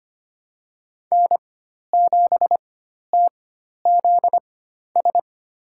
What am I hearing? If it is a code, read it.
Morse code, heard as a keyed tone, about 25 wpm, spaced as for 12 wpm, 700 Hz. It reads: N7TZS
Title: USA Call Signs - 5 Characters - 4x Character Spacing - 25wpm